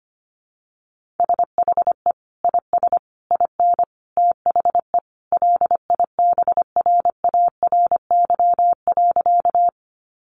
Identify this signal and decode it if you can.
Morse code, heard as a keyed tone, about 25 wpm, 700 Hz. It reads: SHE IS IN THE LIBRARY.
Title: Sentences from Top 600 Words 25wpm